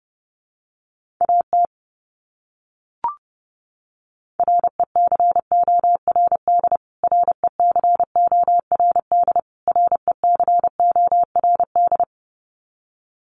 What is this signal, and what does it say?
Morse code, heard as a keyed tone, about 30 wpm, 700 Hz. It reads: AT RECORD RECORD RECORD
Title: Top 500 Words Repeated 30wpm